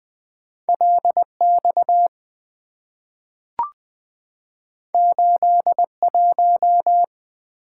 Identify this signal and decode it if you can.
Morse code, heard as a keyed tone, about 20 wpm, 700 Hz. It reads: LX 81
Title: Two Letter-Number Combinations - 20wpm